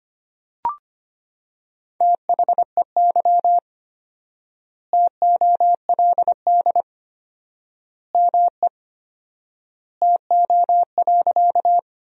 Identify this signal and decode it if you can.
Morse code, heard as a keyed tone, about 25 wpm, 700 Hz. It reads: THEY TOLD ME TO.
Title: 4x Word-Spacing - Sentences from Top 400 Words - 25wpm